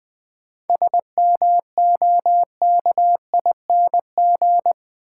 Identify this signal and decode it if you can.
Morse code, heard as a keyed tone, about 21 wpm, 700 Hz. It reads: SMOKING
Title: 7 Letter Words - First 3 Letters Given - 20wpm